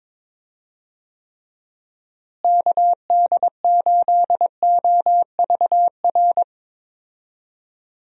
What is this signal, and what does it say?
Morse code, heard as a keyed tone, about 22 wpm, 700 Hz. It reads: KD8OVR
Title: USA Call Signs - 6 Characters - Repeated 2x - 22wpm